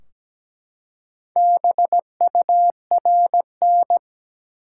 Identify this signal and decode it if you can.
Morse code, heard as a keyed tone, about 17 wpm, 700 Hz. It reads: BURN
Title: Four Letter Words - 17wpm